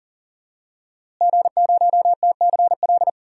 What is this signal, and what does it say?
Morse code, heard as a keyed tone, about 40 wpm, 700 Hz. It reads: G0TCL
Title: DXCC-100 Call Signs - 5 Characters - Repeated 2x - 40wpm